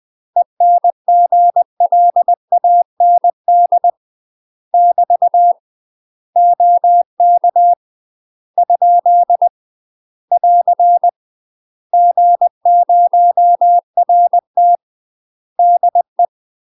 Morse code, heard as a keyed tone, about 20 wpm, 700 Hz. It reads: ENGLAND <BT> OK ? <AR> G0RT DE
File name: UK MASTER QSOs TRAINING / 20WPM / UK MASTER QSOs 0043